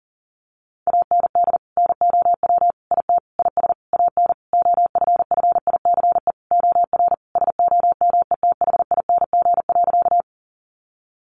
Morse code, heard as a keyed tone, about 40 wpm, 700 Hz. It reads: AND NOW IT IS AN OFFICE OR SOMETHING.
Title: Sentences from Top 1000 Words 40wpm